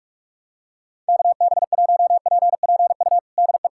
Morse code, heard as a keyed tone, about 46 wpm, 700 Hz. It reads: KB1PPU DE
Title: USA Callsign DE Callsign - 6 Characters - 45wpm